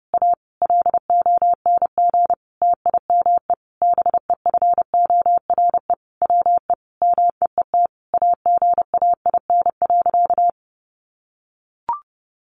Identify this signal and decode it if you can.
Morse code, heard as a keyed tone, about 30 wpm, 700 Hz. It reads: A LONG TIME BEFORE WE MEET AGAIN.
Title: Sentences from Top 1100 Words 30wpm